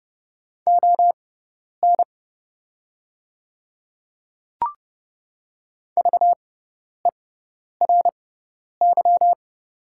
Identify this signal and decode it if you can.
Morse code, heard as a keyed tone, about 31 wpm, spaced as for 10 wpm, 690 Hz. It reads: ON VERY
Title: 6x Character Spacing - Top 200 Words - 30wpm